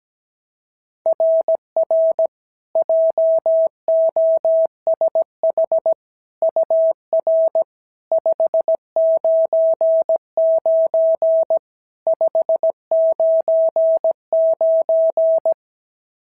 Morse code, heard as a keyed tone, about 16 wpm, 650 Hz. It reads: RR JOSH UR 599 599
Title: SOTA QSO Practice 17wpm